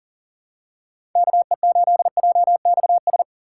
Morse code, heard as a keyed tone, about 37 wpm, 700 Hz. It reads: KE8JXS
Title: USA Call Signs - 8x Character Spacing - 40wpm